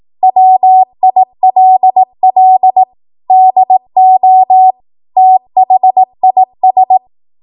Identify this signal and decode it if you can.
Morse code, heard as a keyed tone, about 18 wpm, 750 Hz. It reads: WILL DO THIS
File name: Archive / 18 WPM / 18WPM